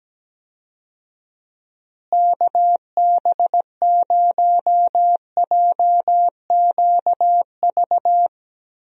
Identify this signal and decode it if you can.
Morse code, heard as a keyed tone, about 17 wpm, 700 Hz. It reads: KB0JQV
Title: USA Call Signs - 6 Characters - Repeated 3x - 17wpm